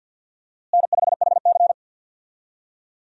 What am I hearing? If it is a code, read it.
Morse code, heard as a keyed tone, about 53 wpm, 700 Hz. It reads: N5HC
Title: USA Call Signs - 4 Characters - Repeated 3x - 50wpm